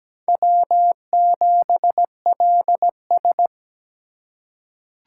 Morse code, heard as a keyed tone, about 17 wpm, 700 Hz. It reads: W7LS
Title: USA Call Signs - 4 Characters - 4x Character Spacing - 17wpm